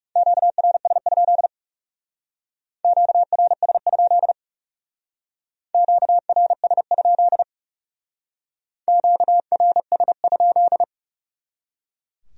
Morse code, heard as a keyed tone, about 38 wpm, 700 Hz. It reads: QRS? QRS? QRS? QRS?